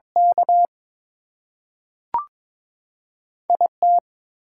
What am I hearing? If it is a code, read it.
Morse code, heard as a keyed tone, about 22 wpm, 700 Hz. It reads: K IT
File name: practice / 22wpm / Two Letter-Number Combinations - 22wpm